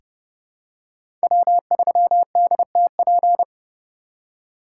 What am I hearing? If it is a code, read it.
Morse code, heard as a keyed tone, about 30 wpm, 700 Hz. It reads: W3DTP
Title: USA Call Signs - 5 Characters - 2x Character Spacing - 30wpm